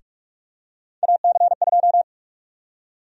Morse code, heard as a keyed tone, about 46 wpm, 700 Hz. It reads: AC2
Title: USA Call Sign Prefixes - 45wpm